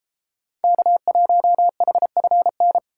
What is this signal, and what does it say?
Morse code, heard as a keyed tone, about 31 wpm, 700 Hz. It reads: K1HFN